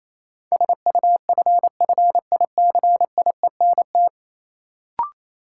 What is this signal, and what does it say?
Morse code, heard as a keyed tone, about 28 wpm, 700 Hz. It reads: SUFFICIENT
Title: Sentences from Top 1400 Words - New Word Drill - 28wpm